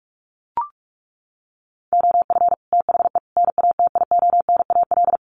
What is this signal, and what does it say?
Morse code, heard as a keyed tone, about 44 wpm, 700 Hz. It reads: OF THE NATIONAL
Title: Sets of 3 Words 45wpm